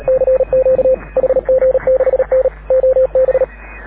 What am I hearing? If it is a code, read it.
Morse code, heard as a keyed tone, about 36 wpm, 530 Hz. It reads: CQ HG6N OB
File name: AUDIO2019 / HG6N CW SSB MIX